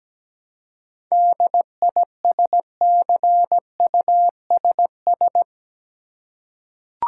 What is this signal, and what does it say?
Morse code, heard as a keyed tone, about 17 wpm, 700 Hz. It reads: DISCUSS
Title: Seven Letter Words - 17wpm